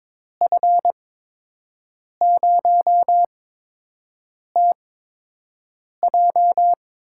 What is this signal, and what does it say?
Morse code, heard as a keyed tone, about 22 wpm, spaced as for 6 wpm, 700 Hz. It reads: F0TJ
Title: USA Call Signs - 5 Characters - 8x Character Spacing - 22wpm